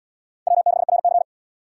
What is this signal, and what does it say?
Morse code, heard as a keyed tone, about 72 wpm, 700 Hz. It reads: W6RC